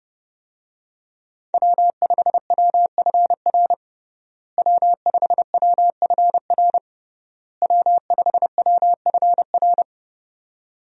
Morse code, heard as a keyed tone, about 30 wpm, 700 Hz. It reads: W5WFR W5WFR W5WFR
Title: USA Call Signs - 5 Characters - Repeated 3x - 30wpm